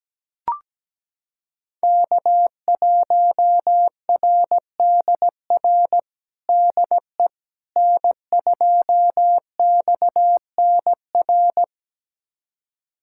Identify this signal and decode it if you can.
Morse code, heard as a keyed tone, about 17 wpm, 700 Hz. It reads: K1RDR DE N2XNR
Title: USA Callsign DE Callsign - 5 Characters - 17wpm